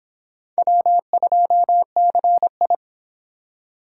Morse code, heard as a keyed tone, about 25 wpm, 700 Hz. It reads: W2CI